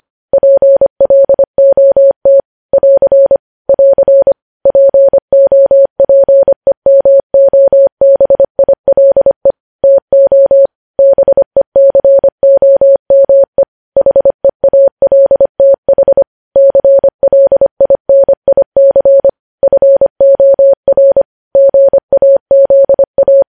Morse code, heard as a keyed tone, about 25 wpm, 550 Hz. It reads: PLOT <AR> <AR> POPEMOBILE TO BECOME HEALTH CLINIC FOR GAZA
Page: news-headlines-in-morse-code-at-25-wpm